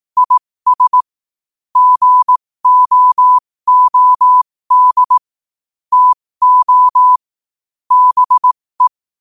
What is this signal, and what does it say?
Morse code, heard as a keyed tone, about 18 wpm, spaced as for 15 wpm, 1000 Hz. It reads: IS GOOD TO BE
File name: 15wpm